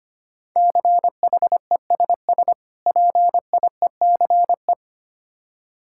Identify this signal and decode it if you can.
Morse code, heard as a keyed tone, about 26 wpm, 700 Hz. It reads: CHESS PIECE
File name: practice / 25wpm / Difficult Words - Speed-Racing with Hints - 25wpm